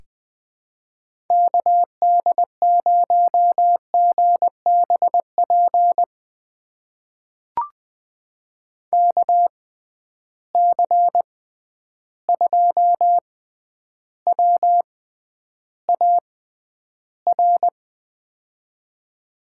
Morse code, heard as a keyed tone, about 20 wpm, spaced as for 10 wpm, 700 Hz. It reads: KD0GBP KC2WAR
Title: USA Call Signs - 6 Characters - 6x Character Spacing - 20wpm